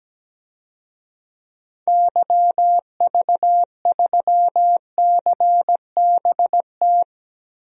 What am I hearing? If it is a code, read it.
Morse code, heard as a keyed tone, about 17 wpm, 700 Hz. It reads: YV3CBT
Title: DXCC-100 Call Signs - 6 Characters - Repeated 3x - 17wpm